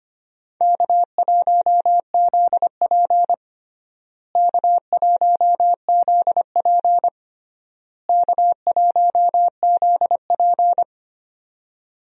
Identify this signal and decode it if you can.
Morse code, heard as a keyed tone, about 25 wpm, 700 Hz. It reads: K1ZP K1ZP K1ZP
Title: USA Call Signs - 4 Characters - Repeated 3x - 25wpm